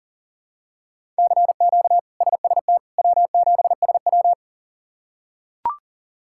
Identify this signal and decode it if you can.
Morse code, heard as a keyed tone, about 40 wpm, 700 Hz. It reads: CQ SST W7SW
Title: K1USN SST QSO Practice - 40wpm